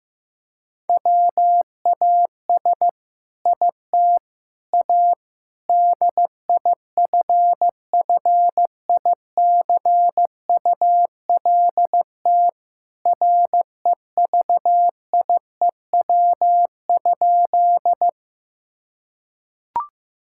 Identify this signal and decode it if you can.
Morse code, heard as a keyed tone, about 15 wpm, 700 Hz. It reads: WAS IT A DIFFICULT REVIEW?
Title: Sentences from Top 1000 Words - Review Part 6 - 15wpm